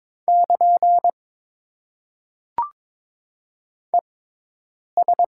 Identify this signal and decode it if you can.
Morse code, heard as a keyed tone, about 22 wpm, spaced as for 8 wpm, 700 Hz. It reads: <KN> ES
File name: Top 90 QSO Elements - 6x Character Spacing - 22wpm